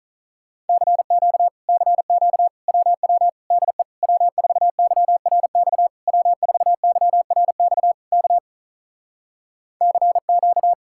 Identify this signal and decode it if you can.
Morse code, heard as a keyed tone, about 48 wpm, 700 Hz. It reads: CQ CQ WW DE W4YRX W4YRX K CQ